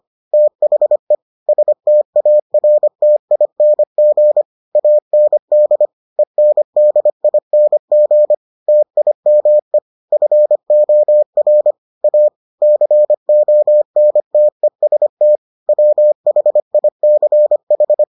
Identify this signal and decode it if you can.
Morse code, heard as a keyed tone, about 25 wpm, 600 Hz. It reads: THE STARTING AND ENDING TIME FOR A CONTEST WHICH